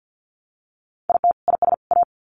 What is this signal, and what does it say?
Morse code, heard as a keyed tone, about 50 wpm, 700 Hz. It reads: IT IS A
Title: Sets of 3 Words 50wpm